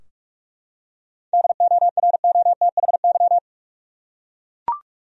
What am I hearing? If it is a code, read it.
Morse code, heard as a keyed tone, about 44 wpm, 700 Hz. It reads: DOROTHY
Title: USA Common Names - 45wpm